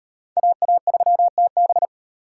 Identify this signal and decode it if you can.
Morse code, heard as a keyed tone, about 35 wpm, 700 Hz. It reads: AA3TB